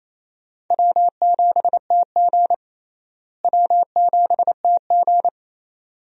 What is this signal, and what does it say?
Morse code, heard as a keyed tone, about 28 wpm, 700 Hz. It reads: W7TG W7TG
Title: USA Call Signs - 4 Characters - Repeated 2x - 28wpm